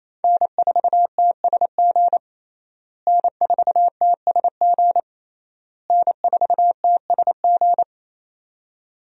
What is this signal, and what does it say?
Morse code, heard as a keyed tone, about 28 wpm, 700 Hz. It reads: N4TSG N4TSG N4TSG
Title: USA Call Signs - 5 Characters - Repeated 3x - 28wpm